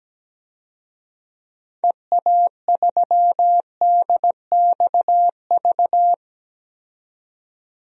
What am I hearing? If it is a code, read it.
Morse code, heard as a keyed tone, about 17 wpm, 700 Hz. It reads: EA3DXV
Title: DXCC-100 Call Signs - 6 Characters - 17wpm